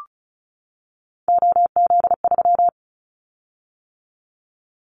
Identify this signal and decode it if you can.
Morse code, heard as a keyed tone, about 34 wpm, 700 Hz. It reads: OZ3
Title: Callsign Prefixes for Top 50 DXCC - 35wpm